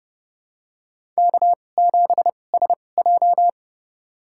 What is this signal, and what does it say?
Morse code, heard as a keyed tone, about 31 wpm, spaced as for 22 wpm, 700 Hz. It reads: K7SJ
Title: USA Call Signs - 4 Characters - 2x Character Spacing - 30wpm